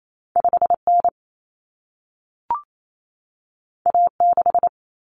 Morse code, heard as a keyed tone, about 28 wpm, 700 Hz. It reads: 5N A6
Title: Common Cut Numbers - 2-Digit 28wpm